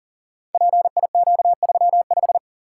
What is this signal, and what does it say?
Morse code, heard as a keyed tone, about 41 wpm, 690 Hz. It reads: PIQ35
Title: Code Groups - 5 Letter:Numbers - 40wpm